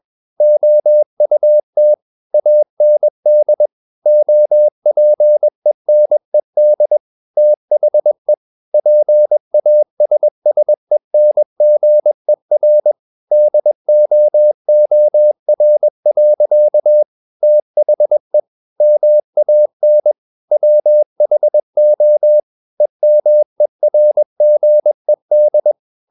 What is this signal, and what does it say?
Morse code, heard as a keyed tone, about 21 wpm, 600 Hz. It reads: OUT AND OPENED THE PASSENGER DOOR. THE MAN WHO EMERGED
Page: CW med Gnister